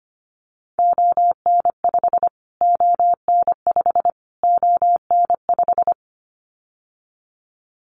Morse code, heard as a keyed tone, about 25 wpm, 700 Hz. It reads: ON5 ON5 ON5
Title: Callsign Prefixes for Top 25 DXCC - Repeated 3x - 25wpm